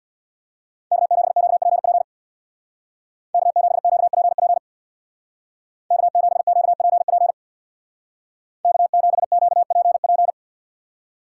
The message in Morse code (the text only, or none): K7QPP K7QPP K7QPP K7QPP